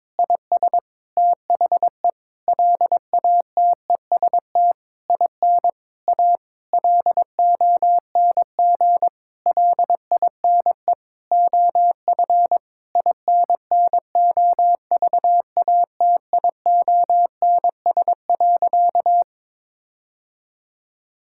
Morse code, heard as a keyed tone, about 22 wpm, 700 Hz. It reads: IS THE LATEST IN A LONG LINE OF INNOVATIONS.
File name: Sentences with 7-Word Phrases - 22wpm